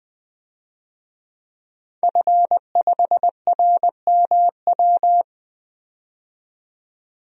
Morse code, heard as a keyed tone, about 21 wpm, 700 Hz. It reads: F5RMW
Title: DXCC-100 Call Signs - 5 Characters - 20wpm